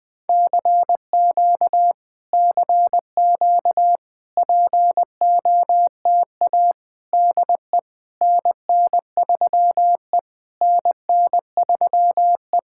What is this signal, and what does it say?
Morse code, heard as a keyed tone, about 20 wpm, 700 Hz. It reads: CQ CQ POTA DE NN3E NN3E
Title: POTA QSO Practice - 20wpm